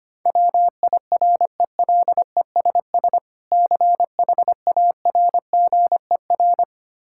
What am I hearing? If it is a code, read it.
Morse code, heard as a keyed tone, about 25 wpm, 700 Hz. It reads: WIRELESS CHARGER
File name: practice / 25wpm / Difficult Words with Hints - 25wpm